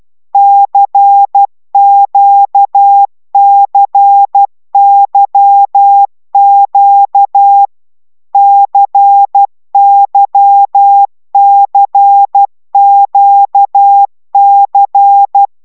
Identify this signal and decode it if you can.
Morse code, heard as a keyed tone, about 12 wpm, 800 Hz. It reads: CQCYQ CYCQC
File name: taviro tanulas / cw 75 15 10